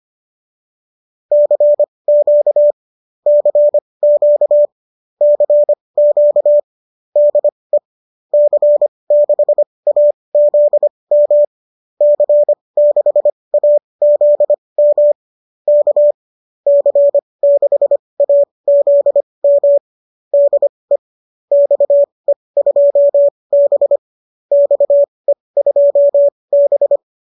Morse code, heard as a keyed tone, about 25 wpm, spaced as for 20 wpm, 590 Hz. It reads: CQ CQ CQ DE C6AZM C6AZM K C6AZM DE XE2B XE2B